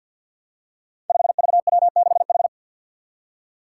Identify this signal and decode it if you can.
Morse code, heard as a keyed tone, about 53 wpm, 700 Hz. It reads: 5VW6H
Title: Five Challenging Letter-Number Combinations 50wpm